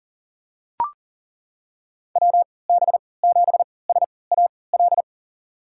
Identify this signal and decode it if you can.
Morse code, heard as a keyed tone, about 40 wpm, spaced as for 23 wpm, 700 Hz. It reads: WB7SAL
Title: USA Call Signs - 3x Character Spacing - 40wpm